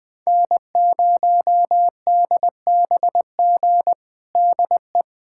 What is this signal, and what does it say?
Morse code, heard as a keyed tone, about 20 wpm, 700 Hz. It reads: N0DBG DE
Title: USA Callsign DE Callsign 20wpm